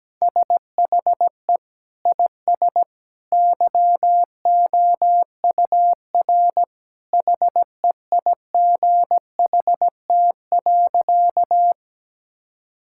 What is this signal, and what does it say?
Morse code, heard as a keyed tone, about 17 wpm, 700 Hz. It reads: SHE IS YOUR HEIGHT.
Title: Sentences from Top 1600 Words 17wpm